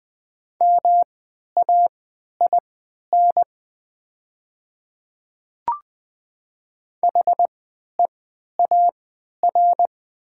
Morse code, heard as a keyed tone, about 20 wpm, spaced as for 11 wpm, 690 Hz. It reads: MAIN HEAR
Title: 3x Character Spacing - Top 300 Words - 20wpm